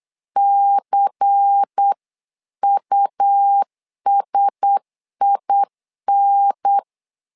Morse code, heard as a keyed tone, about 8 wpm, 800 Hz. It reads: CUSIN
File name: LEZIONI / 5 LBVC / 5 LBVC 8wpm 2935